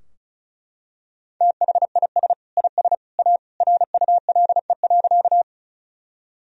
Morse code, heard as a keyed tone, about 36 wpm, 700 Hz. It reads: THIS IS A RULE.
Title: Sentences from Top 1000 Words - Review Part 2 - 35wpm